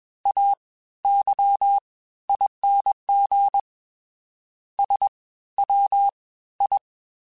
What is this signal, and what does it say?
Morse code, heard as a keyed tone, about 21 wpm, spaced as for 14 wpm, 800 Hz. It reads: AYING SWI